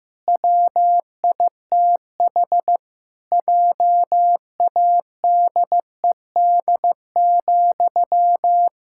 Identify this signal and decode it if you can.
Morse code, heard as a keyed tone, about 15 wpm, 700 Hz. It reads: WITH JADED,
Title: Pangram Challenge v2 - 15wpm